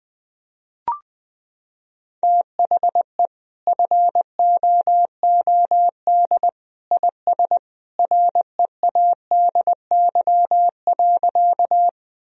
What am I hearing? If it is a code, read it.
Morse code, heard as a keyed tone, about 20 wpm, 700 Hz. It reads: THE FOOD IS READY.
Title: Sentences from Top 1000 Words - Review Part 1 - 20wpm